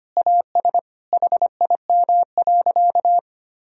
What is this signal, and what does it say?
Morse code, heard as a keyed tone, about 25 wpm, 700 Hz. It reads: AS HIM.
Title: Sentences from Top 1000 Words - Review Part 3 - 25wpm